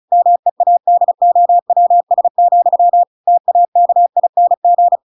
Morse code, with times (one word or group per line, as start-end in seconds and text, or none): 0.1-3.0 MEADOWS,
3.3-5.0 TAKING